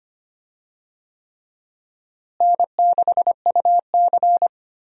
Morse code, heard as a keyed tone, about 25 wpm, 700 Hz. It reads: N6UC
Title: USA Call Signs - 4 Characters - Repeated 3x - 25wpm